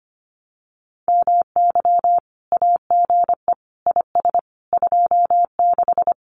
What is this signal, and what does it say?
Morse code, heard as a keyed tone, about 25 wpm, 700 Hz. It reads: MY AGE IS 26
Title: Common QSO Phrases Encore 25wpm